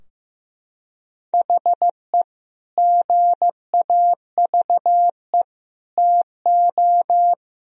Morse code, heard as a keyed tone, about 15 wpm, 700 Hz. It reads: HE GAVE TO
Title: Sentences from Top 1000 Words - Review Part 4 - 15wpm